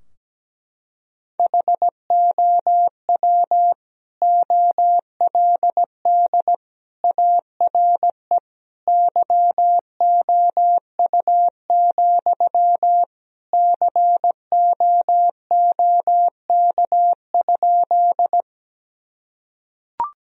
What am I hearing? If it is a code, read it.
Morse code, heard as a keyed tone, about 17 wpm, 700 Hz. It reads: HOW OLD ARE YOU, COOK?
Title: Sentences from Top 1000 Words - Review Part 4 - 17wpm